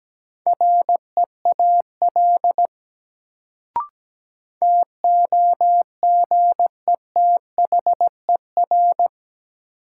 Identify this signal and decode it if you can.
Morse code, heard as a keyed tone, about 17 wpm, 700 Hz. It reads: REAL TOGETHER
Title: Sentences from Top 300 Words - New Word Drill - 17wpm